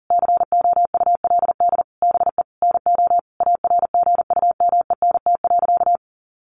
Morse code, heard as a keyed tone, about 40 wpm, 700 Hz. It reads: COULD BE NO ARGUMENT.
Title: A Sound Of Thunder 40wpm